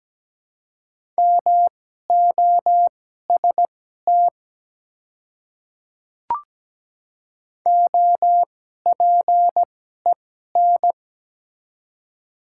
Morse code, heard as a keyed tone, about 17 wpm, spaced as for 12 wpm, 700 Hz. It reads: MOST OPEN
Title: 2x Character Spacing - Top 300 Words - 17wpm